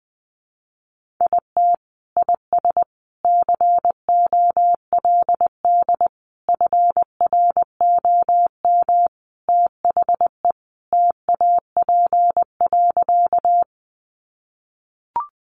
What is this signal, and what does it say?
Morse code, heard as a keyed tone, about 20 wpm, 700 Hz. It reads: IT IS COLD FROM THE TAP.
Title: Sentences from Top 1400 Words 20wpm